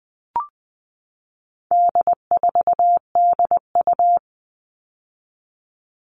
Morse code, heard as a keyed tone, about 20 wpm, 700 Hz. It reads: D4DU
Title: All Cut Numbers - 4-Digit 20wpm